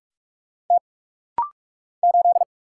Morse code, heard as a keyed tone, about 46 wpm, 700 Hz. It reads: T 8
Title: Single Letter-Number Alt - Rapid-Fire 45wpm